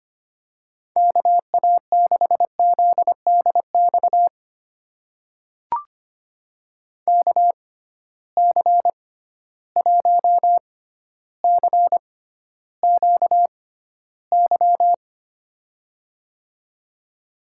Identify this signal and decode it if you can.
Morse code, heard as a keyed tone, about 25 wpm, spaced as for 13 wpm, 700 Hz. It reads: KA6ZDX KC1CQY
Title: USA Call Signs - 6 Characters - 6x Character Spacing - 25wpm